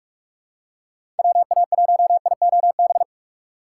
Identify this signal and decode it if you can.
Morse code, heard as a keyed tone, about 46 wpm, 700 Hz. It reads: WA1IOB
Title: USA Call Signs - 6 Characters - 4x Character Spacing - 45wpm